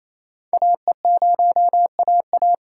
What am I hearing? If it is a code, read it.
Morse code, heard as a keyed tone, about 28 wpm, 700 Hz. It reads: AE0AA
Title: USA Call Signs - 5 Characters - 2x Character Spacing - 28wpm